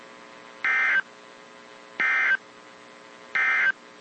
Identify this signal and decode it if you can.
Emergency Alert System (SAME) tones, heard as harsh buzzing bursts. It is heard from 0.5 to 3.8 s.
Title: WZ2523 - NOAA Weather Radio